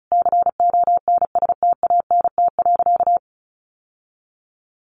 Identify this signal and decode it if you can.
Morse code, heard as a keyed tone, about 34 wpm, 700 Hz. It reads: CONSTANT.
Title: Sentences from Top 1500 Words 35wpm